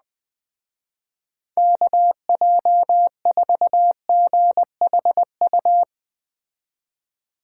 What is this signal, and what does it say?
Morse code, heard as a keyed tone, about 20 wpm, 700 Hz. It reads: KJ4GHU
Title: Speed Racing - USA 2x3 Call Signs 17wpm